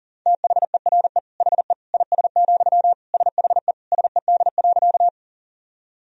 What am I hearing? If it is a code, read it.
Morse code, heard as a keyed tone, about 40 wpm, 700 Hz. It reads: THERE HE IS, SHE SED.
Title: Sentences from Top 100 Words and Abbreviations 40wpm